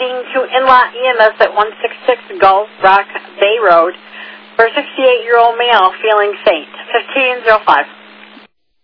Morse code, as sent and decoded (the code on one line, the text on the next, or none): none